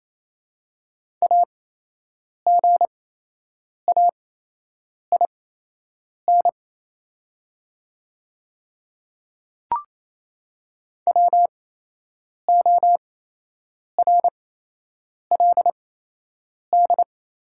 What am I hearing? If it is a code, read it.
Morse code, heard as a keyed tone, about 28 wpm, spaced as for 8 wpm, 700 Hz. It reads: AGAIN WORLD
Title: 8x Character Spacing - Top 200 Words - 28wpm